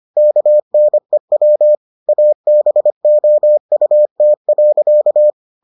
Morse code, heard as a keyed tone, about 25 wpm, 600 Hz. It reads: KNEW ABOUT.